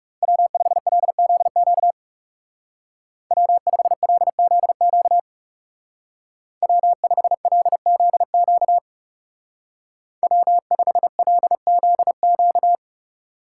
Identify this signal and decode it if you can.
Morse code, heard as a keyed tone, about 39 wpm, 700 Hz. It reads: W5LZQ W5LZQ W5LZQ W5LZQ